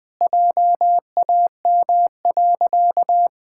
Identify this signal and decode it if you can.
Morse code, heard as a keyed tone, about 20 wpm, 700 Hz. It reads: JAM.
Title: Pangram Challenge v2 - 20wpm